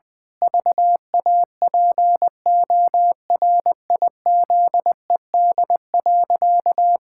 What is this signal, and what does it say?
Morse code, heard as a keyed tone, about 20 wpm, 700 Hz. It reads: VAPORIZED.
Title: Pangram Challenge v2 - 20wpm